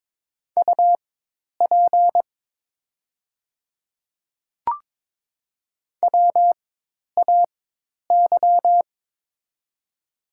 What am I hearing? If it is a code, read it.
Morse code, heard as a keyed tone, about 22 wpm, spaced as for 10 wpm, 700 Hz. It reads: UP WAY
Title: 4x Character Spacing - Top 300 Words - 22wpm